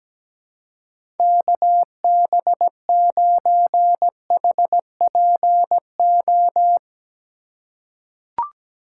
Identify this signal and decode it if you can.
Morse code, heard as a keyed tone, about 17 wpm, 700 Hz. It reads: KB9HPO
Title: USA Call Signs - 6 Characters - 4x Character Spacing - 17wpm